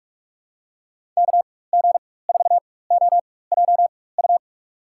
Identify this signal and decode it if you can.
Morse code, heard as a keyed tone, about 47 wpm, spaced as for 21 wpm, 700 Hz. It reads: KG4OJU
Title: USA Call Signs - 6 Characters - 4x Character Spacing - 45wpm